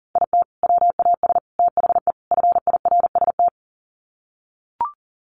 Morse code, heard as a keyed tone, about 40 wpm, 700 Hz. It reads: IT WAS THE FIRST
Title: Sets of 4 Words 40wpm